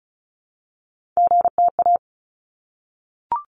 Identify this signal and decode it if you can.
Morse code, heard as a keyed tone, about 36 wpm, 700 Hz. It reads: GTA
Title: ARRL Sections 35wpm